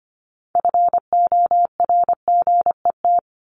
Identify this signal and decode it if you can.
Morse code, heard as a keyed tone, about 25 wpm, 700 Hz. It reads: FORGET
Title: Top 1001-1250 Words 25wpm